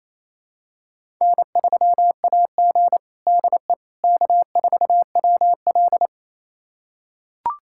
Morse code, heard as a keyed tone, about 28 wpm, 700 Hz. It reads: N3AG DE K4WL
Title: USA Callsign DE Callsign - 4 Characters - 28wpm